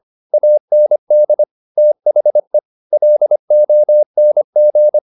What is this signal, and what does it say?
Morse code, heard as a keyed tone, about 25 wpm, 600 Hz. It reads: AND THE LONG
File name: the old man and the sea part 1 25wpm